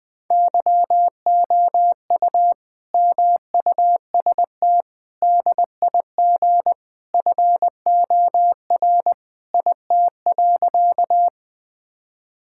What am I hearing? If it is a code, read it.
Morse code, heard as a keyed tone, about 20 wpm, 700 Hz. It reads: YOU MUST DIG FOR IT.
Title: Sentences from Top 1600 Words 20wpm